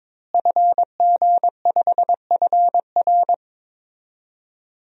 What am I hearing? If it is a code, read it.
Morse code, heard as a keyed tone, about 22 wpm, 700 Hz. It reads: FG5FR
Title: DXCC-100 Call Signs - 5 Characters - Repeated 2x - 22wpm